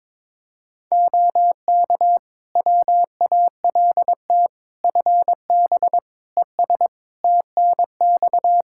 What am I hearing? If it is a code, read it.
Morse code, heard as a keyed tone, about 22 wpm, 700 Hz. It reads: OK WALT FB ES TNX
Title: Practice QSO - Exchange 2 - Info - 22wpm